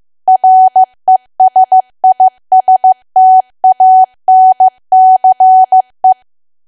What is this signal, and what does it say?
Morse code, heard as a keyed tone, about 15 wpm, 750 Hz. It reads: RESISTANCE
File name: cwa / W1AW-Level2 / 15WPM